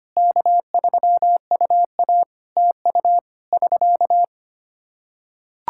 Morse code, heard as a keyed tone, about 25 wpm, 700 Hz. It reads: K3UA TU <SK>
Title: Practice QSO - Exchange 4 - Ending - 25wpm